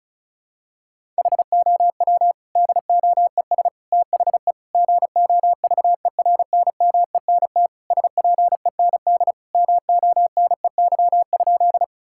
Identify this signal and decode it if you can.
Morse code, heard as a keyed tone, about 34 wpm, 700 Hz. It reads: HOW DOES THE GOVERNMENT SPEND MONEY?
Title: Sentences from Top 1000 Words - Review Part 5 - 35wpm